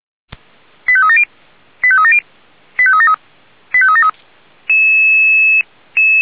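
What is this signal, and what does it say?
Five-tone selective call (ZVEI), heard as quick runs of five tones, tones of about 70 ms. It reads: 85370 85370 85373 85373
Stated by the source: Five-tone ZVEI selective call received by SDR, logged on the audio server of Feuerwehr Orlamünde (f85360).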